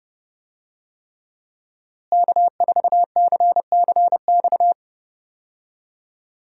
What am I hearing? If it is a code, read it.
Morse code, heard as a keyed tone, about 31 wpm, 700 Hz. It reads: K4CCX
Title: USA Call Signs - 5 Characters - Repeated 3x - 30wpm